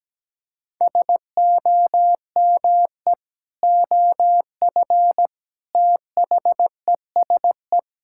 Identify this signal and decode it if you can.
Morse code, heard as a keyed tone, about 17 wpm, 700 Hz. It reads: SOME OF THESE
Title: Sentences from Top 1000 Words - Review Part 6 - 17wpm